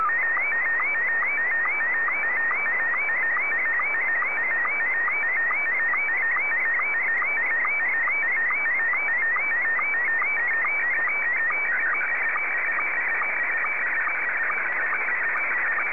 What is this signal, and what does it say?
SSTV (slow-scan television), Scottie 1.